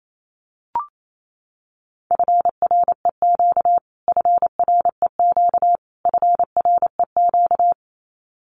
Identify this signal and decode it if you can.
Morse code, heard as a keyed tone, about 28 wpm, 700 Hz. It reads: FREQ FREQ FREQ
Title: Abbreviations Repeated 3x - 28wpm